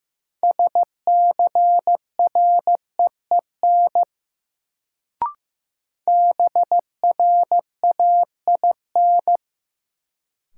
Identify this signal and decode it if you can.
Morse code, heard as a keyed tone, about 16 wpm, 700 Hz. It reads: SCREEN BRAIN
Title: Sentences from Top 1000 Words - Review Part 6 - New Word Drill - 15wpm